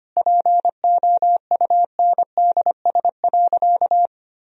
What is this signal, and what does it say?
Morse code, heard as a keyed tone, about 25 wpm, 700 Hz. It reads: POUNDS.
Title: Sentences from Top 1000 Words - Review Part 2 - 25wpm